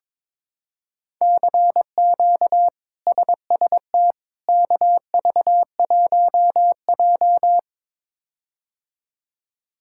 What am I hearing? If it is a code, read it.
Morse code, heard as a keyed tone, about 22 wpm, 700 Hz. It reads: CQ SST KV1J
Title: K1USN SST QSO Practice - 22wpm